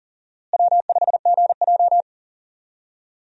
Morse code, heard as a keyed tone, about 38 wpm, 700 Hz. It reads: W5GJ
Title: USA Call Signs - 4 Characters - Repeated 3x - 40wpm